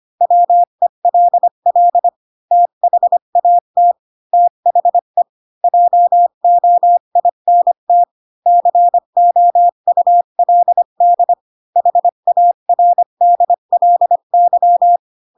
Morse code, heard as a keyed tone, about 25 wpm, 700 Hz. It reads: WELL THAT THE JOINT COULD HARDLY
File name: Pinocchio / Pinocchio ch8 20 25 ik1hge